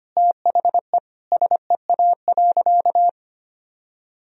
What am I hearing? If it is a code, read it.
Morse code, heard as a keyed tone, about 25 wpm, 700 Hz. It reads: THE SEA.
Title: Sentences from Top 1000 Words - Review Part 1 - 25wpm